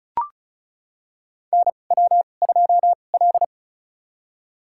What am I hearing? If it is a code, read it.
Morse code, heard as a keyed tone, about 34 wpm, spaced as for 25 wpm, 700 Hz. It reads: NW2L
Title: USA Call Signs - 4 Characters - 2x Character Spacing - 35wpm